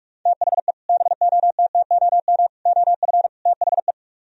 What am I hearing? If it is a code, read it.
Morse code, heard as a keyed tone, about 47 wpm, 700 Hz. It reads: THE BOTTOM OF THE